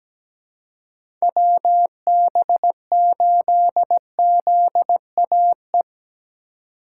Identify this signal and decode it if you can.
Morse code, heard as a keyed tone, about 17 wpm, 700 Hz. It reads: WB8ZAE
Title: USA Call Signs - 6 Characters - 3x Character Spacing - 17wpm